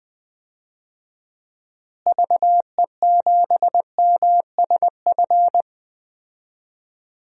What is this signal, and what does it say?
Morse code, heard as a keyed tone, about 20 wpm, 700 Hz. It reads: VE7MSF
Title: DXCC-100 Call Signs - 6 Characters - Repeated 2x - 20wpm